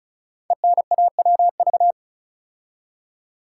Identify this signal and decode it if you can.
Morse code, heard as a keyed tone, about 36 wpm, 690 Hz. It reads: ENAWV
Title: Code Groups - 5 Letters - 35wpm